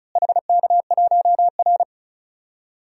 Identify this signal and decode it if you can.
Morse code, heard as a keyed tone, about 36 wpm, 700 Hz. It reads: HK1R
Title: DXCC-100 Call Signs - 4 Characters - Repeated 3x - 35wpm